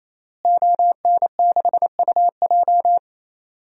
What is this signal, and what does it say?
Morse code, heard as a keyed tone, about 28 wpm, 700 Hz. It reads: ON6UJ
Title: DXCC-100 Call Signs - 5 Characters - Repeated 3x - 28wpm